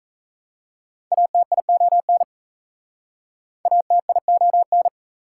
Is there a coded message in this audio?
Yes: Morse code from 1.1 to 4.9 s, heard as a keyed tone, about 36 wpm, 700 Hz. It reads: ATION ATION